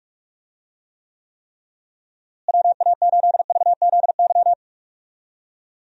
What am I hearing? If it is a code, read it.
Morse code, heard as a keyed tone, about 46 wpm, 700 Hz. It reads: WA8VZY
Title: USA Call Signs - 6 Characters - Repeated 3x - 45wpm